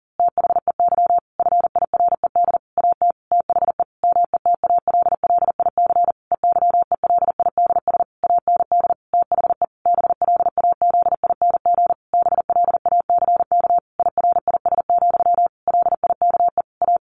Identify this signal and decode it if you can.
Morse code, heard as a keyed tone, about 37 wpm, 700 Hz. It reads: THEY FIRED AT THE METALLIC EYELIDS AND THE BLAZING BLACK IRIS, LIKE A
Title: A Sound Of Thunder 40wpm